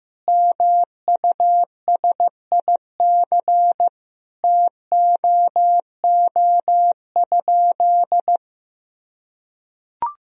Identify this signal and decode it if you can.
Morse code, heard as a keyed tone, about 15 wpm, 700 Hz. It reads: MUSIC TOO?
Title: Sentences from Top 1000 Words - Review Part 1 - 15wpm